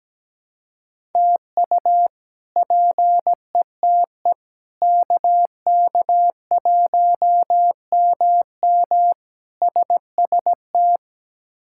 Morse code, heard as a keyed tone, about 17 wpm, 700 Hz. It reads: TU PETE KK1MM SST
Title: K1USN SST QSO Practice - 17wpm